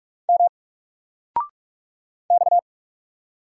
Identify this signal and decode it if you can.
Morse code, heard as a keyed tone, about 44 wpm, 700 Hz. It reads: M X
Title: Single Letter-Number 45wpm